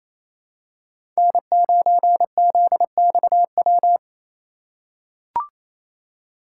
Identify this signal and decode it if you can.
Morse code, heard as a keyed tone, about 27 wpm, 700 Hz. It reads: N9ZXW
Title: USA Call Signs - 5 Characters - 4x Character Spacing - 28wpm